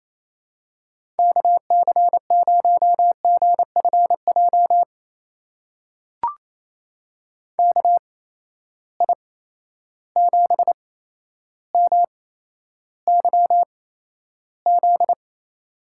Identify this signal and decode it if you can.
Morse code, heard as a keyed tone, about 28 wpm, spaced as for 12 wpm, 700 Hz. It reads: KC0GFJ KI7MYZ
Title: USA Call Signs - 6 Characters - 8x Character Spacing - 28wpm